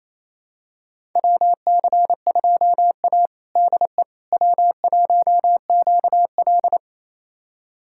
Morse code, heard as a keyed tone, about 28 wpm, 700 Hz. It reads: WC2A DE W1QL
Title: USA Callsign DE Callsign - 4 Characters - 28wpm